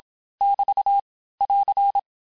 Morse code, heard as a keyed tone, about 27 wpm, 800 Hz. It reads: <BT> <AR>
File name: cours-du-mercredi / audios / cours-869-texte-LES-MOTS-vitesse-25-MPM